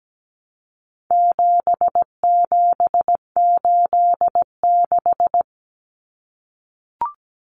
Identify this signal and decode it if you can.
Morse code, heard as a keyed tone, about 17 wpm, 700 Hz. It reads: 7786
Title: Numbers - 4-Digit 17wpm